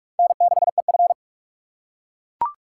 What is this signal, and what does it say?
Morse code, heard as a keyed tone, about 44 wpm, 700 Hz. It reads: N6EF
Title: USA Call Signs - 4 Characters - 6x Character Spacing - 45wpm